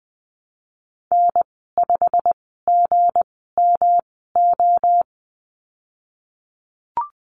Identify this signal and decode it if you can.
Morse code, heard as a keyed tone, about 20 wpm, spaced as for 15 wpm, 700 Hz. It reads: N5GMO
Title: 207 Lesson 33g - Letter j - Call Signs 20wpm